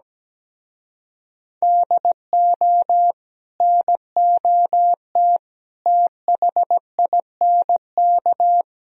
Morse code, heard as a keyed tone, about 17 wpm, 700 Hz. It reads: DO NOT THINK
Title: Speed-Racing - Sets of 4 Words 17wpm